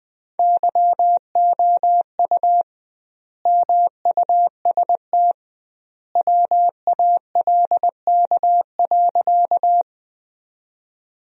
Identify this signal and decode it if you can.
Morse code, heard as a keyed tone, about 21 wpm, 700 Hz. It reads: YOU MUST WALK.
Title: 2x Word-Spacing - Sentences from Top 300 Words - 20wpm